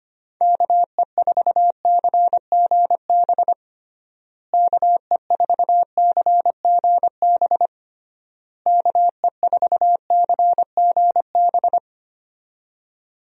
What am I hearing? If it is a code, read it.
Morse code, heard as a keyed tone, about 25 wpm, 700 Hz. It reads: KE4CGB KE4CGB KE4CGB
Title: USA Call Signs - 6 Characters - Repeated 3x - 25wpm